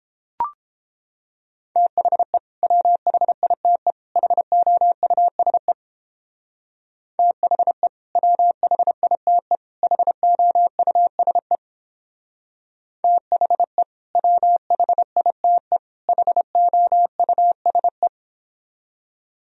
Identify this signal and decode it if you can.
Morse code, heard as a keyed tone, about 31 wpm, 700 Hz. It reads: THE WHITE HOUSE THE WHITE HOUSE THE WHITE HOUSE